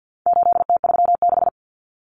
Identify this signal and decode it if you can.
Morse code, heard as a keyed tone, about 50 wpm, 700 Hz. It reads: 8T36
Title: Common Cut Numbers - 4-Digit 50wpm